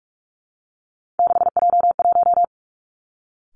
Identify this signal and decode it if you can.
Morse code, heard as a keyed tone, about 46 wpm, 700 Hz. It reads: 6J1
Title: Three Challenging Letter-Number Combinations 45wpm